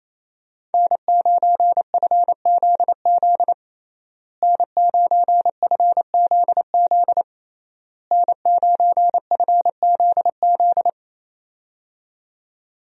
Morse code, heard as a keyed tone, about 27 wpm, 700 Hz. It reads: N9FZZ N9FZZ N9FZZ
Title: USA Call Signs - 5 Characters - Repeated 3x - 28wpm